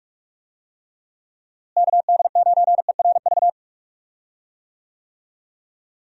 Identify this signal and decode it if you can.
Morse code, heard as a keyed tone, about 46 wpm, 700 Hz. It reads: KD9ERV
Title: USA Call Signs - 6 Characters - Repeated 3x - 45wpm